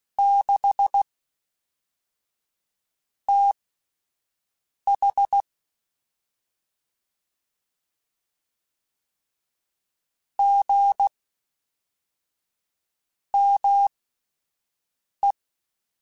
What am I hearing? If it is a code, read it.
Morse code, heard as a keyed tone, about 16 wpm, spaced as for 4 wpm, 800 Hz. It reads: 6TH GME
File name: CW Comparisons / CW Comparisons-mp3s / CW Comparisons Broken Characters @10